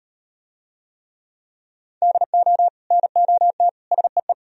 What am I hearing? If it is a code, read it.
Morse code, heard as a keyed tone, about 36 wpm, 700 Hz. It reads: DO NOT SEE